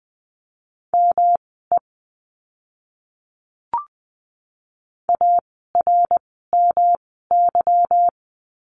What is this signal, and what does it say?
Morse code, heard as a keyed tone, about 20 wpm, spaced as for 15 wpm, 700 Hz. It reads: ME ARMY
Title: 146 Lesson 24f - Letter y - Words 20wpm